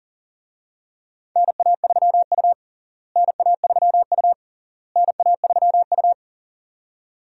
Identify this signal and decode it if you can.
Morse code, heard as a keyed tone, about 40 wpm, 700 Hz. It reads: NA3U NA3U NA3U
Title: USA Call Signs - 4 Characters - Repeated 3x - 40wpm